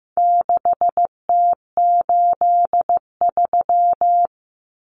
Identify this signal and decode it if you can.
Morse code, heard as a keyed tone, about 15 wpm, 700 Hz. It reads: 6T83
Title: Common Cut Numbers - 4-Digit 15wpm